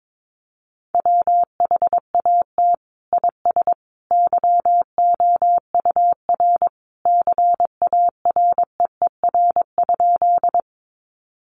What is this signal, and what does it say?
Morse code, heard as a keyed tone, about 22 wpm, 700 Hz. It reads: WHAT IS YOUR CAREER?
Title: Sentences from Top 1000 Words 22wpm